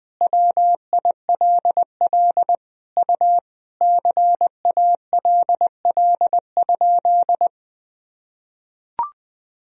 Morse code, heard as a keyed tone, about 20 wpm, 700 Hz. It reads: WILL U CALL?
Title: Sentences from Top 100 Words and Abbreviations 20wpm